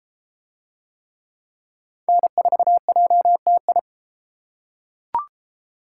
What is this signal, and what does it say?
Morse code, heard as a keyed tone, about 30 wpm, 700 Hz. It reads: N4JTI